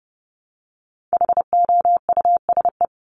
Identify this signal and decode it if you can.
Morse code, heard as a keyed tone, about 30 wpm, 700 Hz. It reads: HOUSE
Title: Top 200 Words Repeated 30wpm